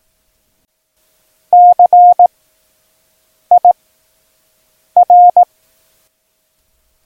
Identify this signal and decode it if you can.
Morse code, heard as a keyed tone, about 18 wpm, 710 Hz. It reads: C I R
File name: acs / Instrument Rating / Instrument ACS